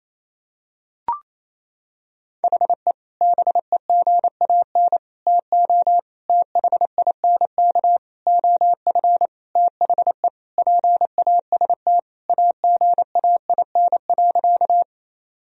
Morse code, heard as a keyed tone, about 27 wpm, 700 Hz. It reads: HE BEGAN TO THINK OF THE PAST AGAIN.
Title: Sentences from Top 1000 Words - Review Part 3 - 28wpm